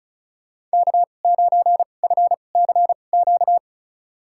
Morse code, heard as a keyed tone, about 36 wpm, spaced as for 25 wpm, 700 Hz. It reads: K9FCQ
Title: USA Call Signs - 5 Characters - 2x Character Spacing - 35wpm